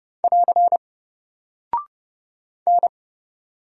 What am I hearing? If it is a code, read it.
Morse code, heard as a keyed tone, about 30 wpm, 700 Hz. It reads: <AR> N
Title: Single Character (Letter:Number:Punctuation:Prosign) Alt 30wpm